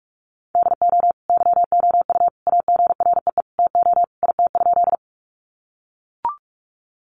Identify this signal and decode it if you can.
Morse code, heard as a keyed tone, about 44 wpm, 700 Hz. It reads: DO YOU AGREE TO IT?
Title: Sentences from Top 1100 Words 45wpm